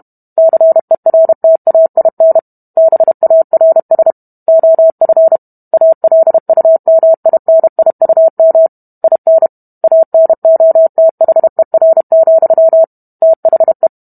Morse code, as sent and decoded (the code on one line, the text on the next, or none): -.-. .-. - .- .. -. / -... .- .-. ... / --- ..-. / .- .-.. ..- -- .. -. .. ..- -- / .. -. / .- -. --- - .... .-. --..-- / - .... .
CERTAIN BARS OF ALUMINIUM IN ANOTHER, THE